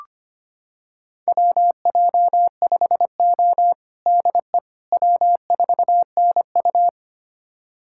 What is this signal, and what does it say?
Morse code, heard as a keyed tone, about 25 wpm, 700 Hz. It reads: WJ5O DE W4NU
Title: USA Callsign DE Callsign - 4 Characters - 25wpm